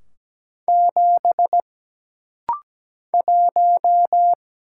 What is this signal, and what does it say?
Morse code, heard as a keyed tone, about 17 wpm, 700 Hz. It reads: 7 1
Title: Single Letter-Number - Rapid-Fire 17wpm